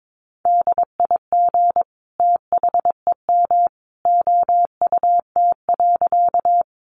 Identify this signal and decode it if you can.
Morse code, heard as a keyed tone, about 22 wpm, 700 Hz. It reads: DIG THEM OUT.
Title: A Sound Of Thunder 22wpm